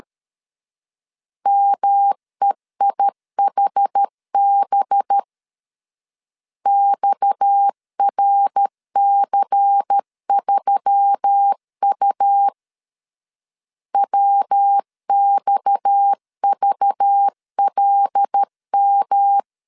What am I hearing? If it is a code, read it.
Morse code, heard as a keyed tone, about 13 wpm, 800 Hz. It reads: MEIHB XRC3U WXVLM
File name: LEZIONI / 6 PKX0 / 6 PKX 12wpm 2024